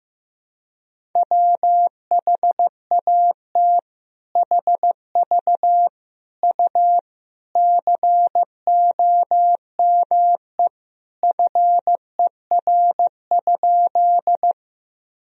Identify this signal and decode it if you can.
Morse code, heard as a keyed tone, about 15 wpm, 700 Hz. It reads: WHAT HV U COME FER?
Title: Sentences from Top 100 Words and Abbreviations 15wpm